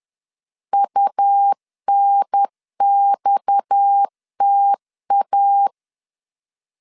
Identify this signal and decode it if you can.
Morse code, heard as a keyed tone, about 10 wpm, 800 Hz. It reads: UNXTA
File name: LEZIONI / 6 PKX0 / 6 PKX 10wpm 2426